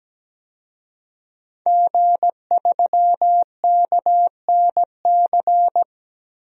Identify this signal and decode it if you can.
Morse code, heard as a keyed tone, about 17 wpm, 700 Hz. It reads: G3KNC
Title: DXCC-100 Call Signs - 5 Characters - Repeated 3x - 17wpm